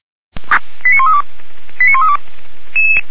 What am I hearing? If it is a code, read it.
Five-tone selective call (ZVEI), heard as quick runs of five tones, tones of about 80 ms. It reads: 89132 89132
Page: /464/SDR/ZVEI/